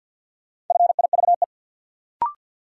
Morse code, heard as a keyed tone, about 52 wpm, 700 Hz. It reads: FIVE
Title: Sets of 5 Words - New Word Drill - 50wpm